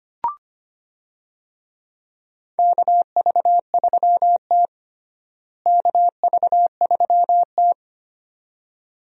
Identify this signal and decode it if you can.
Morse code, heard as a keyed tone, about 25 wpm, 700 Hz. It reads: KV3T KV3T
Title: USA Call Signs - 4 Characters - Repeated 2x - 25wpm